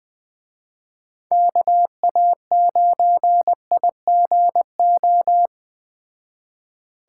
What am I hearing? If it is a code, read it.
Morse code, heard as a keyed tone, about 21 wpm, 700 Hz. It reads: KA9IGO